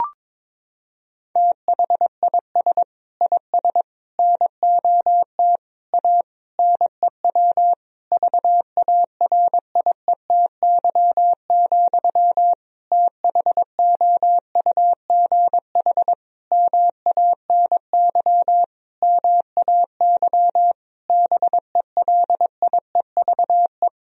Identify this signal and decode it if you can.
Morse code, heard as a keyed tone, about 21 wpm, 700 Hz. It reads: THIS IS NOT A NEW VARIETY, THOUGH MANY MAY BELIEVE
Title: Sentences from Top 1000 Words - Review Part 4 - 22wpm